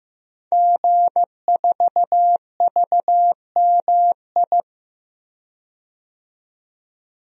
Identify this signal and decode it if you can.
Morse code, heard as a keyed tone, about 15 wpm, 700 Hz. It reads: G4VMI
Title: DXCC-100 Call Signs - 5 Characters - 15wpm